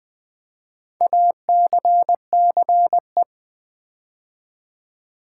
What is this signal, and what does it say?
Morse code, heard as a keyed tone, about 20 wpm, 700 Hz. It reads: ACCE
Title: Build A Word - 4 Letters At A Time - 10-Letter Words - 20wpm